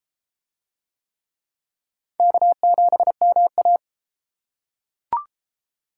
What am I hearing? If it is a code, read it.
Morse code, heard as a keyed tone, about 30 wpm, 700 Hz. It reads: K7MA